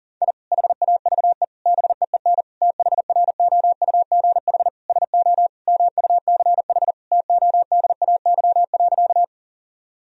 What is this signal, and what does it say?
Morse code, heard as a keyed tone, about 40 wpm, 700 Hz. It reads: I HAVE BEEN THROUGH SO MUCH TODAY.
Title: Sentences from Top 1000 Words - Review Part 4 - 40wpm